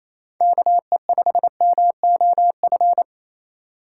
Morse code, heard as a keyed tone, about 29 wpm, 700 Hz. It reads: KE5MOF
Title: Speed Racing - USA Call Signs 28wpm